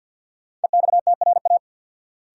Morse code, heard as a keyed tone, about 50 wpm, 700 Hz. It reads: EXTRA
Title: Sentences from Top 1000 Words - New Word Drill - 50wpm